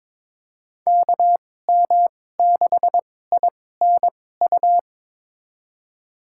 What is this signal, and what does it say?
Morse code, heard as a keyed tone, about 22 wpm, spaced as for 16 wpm, 700 Hz. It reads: KM6INU
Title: USA Call Signs - 6 Characters - 2x Character Spacing - 22wpm